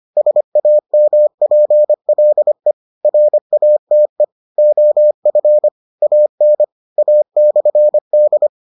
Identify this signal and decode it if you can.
Morse code, heard as a keyed tone, about 25 wpm, 600 Hz. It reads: SAMPLE RATE OF AN A/D